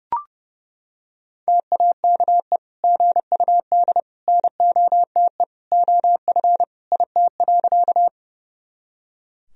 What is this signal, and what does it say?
Morse code, heard as a keyed tone, about 30 wpm, 700 Hz. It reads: TAKE GUD NOTE OF IT.
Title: Sentences from Top 500 Words and Abbreviations 30wpm